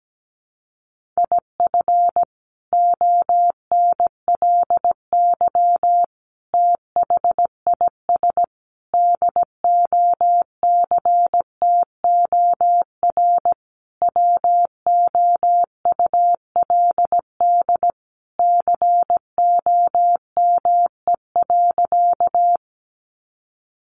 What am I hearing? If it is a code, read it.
Morse code, heard as a keyed tone, about 17 wpm, 700 Hz. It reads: IF ONLY THIS DOCTOR WOULD COME.
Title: Sentences from Top 1500 Words 17wpm